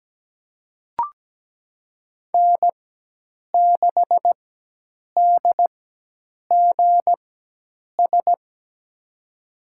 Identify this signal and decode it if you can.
Morse code, heard as a keyed tone, about 17 wpm, spaced as for 8 wpm, 700 Hz. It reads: N6DGS
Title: USA Call Signs - 5 Characters - 4x Character Spacing - 17wpm